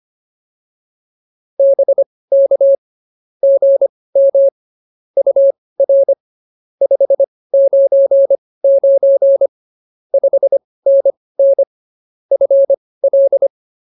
Morse code, heard as a keyed tone, about 24 wpm, spaced as for 18 wpm, 550 Hz. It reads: BK GM UR 599 5NN FL
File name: POTA203 18